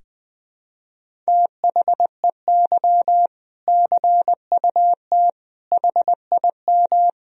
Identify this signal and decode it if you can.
Morse code, heard as a keyed tone, about 20 wpm, 700 Hz. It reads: THEY CUT HIM
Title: Sentences from Top 1000 Words - Review Part 1 - 20wpm